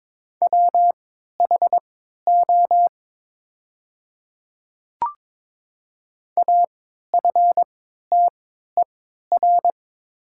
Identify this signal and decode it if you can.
Morse code, heard as a keyed tone, about 22 wpm, spaced as for 12 wpm, 700 Hz. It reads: WHO AFTER
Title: 3x Character Spacing - Top 200 Words - 22wpm